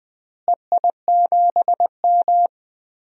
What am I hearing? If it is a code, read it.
Morse code, heard as a keyed tone, about 20 wpm, 690 Hz. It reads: EI7M
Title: DXCC-100 Call Signs - 4 Characters - Repeated 3x - 20wpm